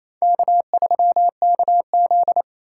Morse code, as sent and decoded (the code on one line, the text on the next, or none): -.- ...-- -.- --..
K3KZ